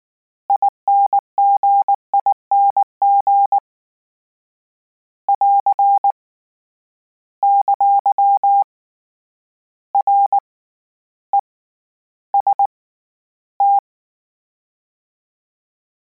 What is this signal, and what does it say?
Morse code, heard as a keyed tone, about 19 wpm, 800 Hz. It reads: INGING <AR> ! R E S T
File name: Combinations / Combinations-mp3s / Combinations@18